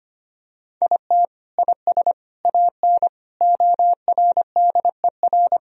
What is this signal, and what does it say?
Morse code, heard as a keyed tone, about 25 wpm, 700 Hz. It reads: IT IS AN ORDER
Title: Sentences from Top 1000 Words - Review Part 2 - 25wpm